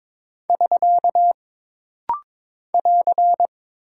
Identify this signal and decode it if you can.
Morse code, heard as a keyed tone, about 22 wpm, 700 Hz. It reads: <SK> <AR>
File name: practice / 22wpm / Single Punctuation-Prosign - Rapid-Fire - 22wpm